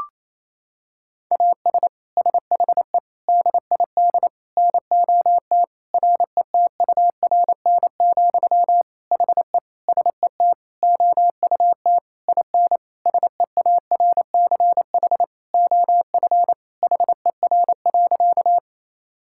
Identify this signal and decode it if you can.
Morse code, heard as a keyed tone, about 28 wpm, 700 Hz. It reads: AS SHE DID NOT RETURN, HE SET OUT IN SEARCH OF HER.
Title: Sentences from Top 1000 Words - Review Part 5 - 28wpm